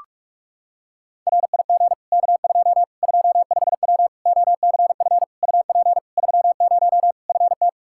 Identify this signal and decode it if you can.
Morse code, heard as a keyed tone, about 41 wpm, 700 Hz. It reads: RIG K2 25W OCF UP 30 FT
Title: Practice QSO - Exchange 2 - Info - 45wpm